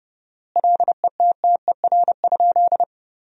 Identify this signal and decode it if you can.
Morse code, heard as a keyed tone, about 30 wpm, 700 Hz. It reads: LETTER?
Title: 4x Word-Spacing - Sentences from Top 300 Words - 30wpm